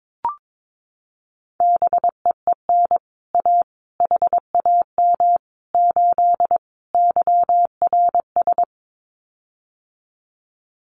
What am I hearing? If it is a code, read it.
Morse code, heard as a keyed tone, about 22 wpm, 700 Hz. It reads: BEEN A HAM 8 YRS
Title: Common QSO Phrases Encore 22wpm